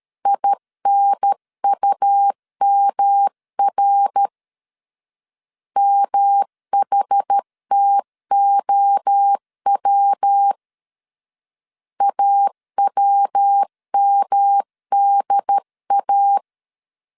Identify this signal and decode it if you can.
Morse code, heard as a keyed tone, about 13 wpm, 800 Hz. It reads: INUMR MHTOW AWMDA
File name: CW / DET / 1 3 1500C S A 12wpm 2905